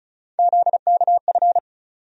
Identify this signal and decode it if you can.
Morse code, heard as a keyed tone, about 36 wpm, 700 Hz. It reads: ZKF
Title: USA Call Sign Postfixes - 3 Letters 35wpm